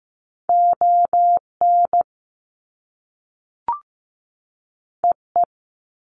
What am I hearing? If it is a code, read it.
Morse code, heard as a keyed tone, about 15 wpm, 700 Hz. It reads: ON EE
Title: Top Two Letter Combinations in Words 15wpm